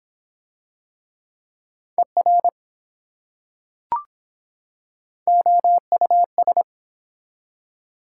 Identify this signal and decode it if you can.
Morse code, heard as a keyed tone, about 24 wpm, 700 Hz. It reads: ER OUS